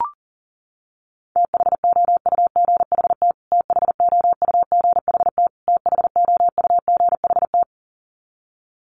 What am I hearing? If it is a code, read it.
Morse code, heard as a keyed tone, about 40 wpm, 700 Hz. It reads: THOUGHT THOUGHT THOUGHT
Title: Top 500 Words Repeated 40wpm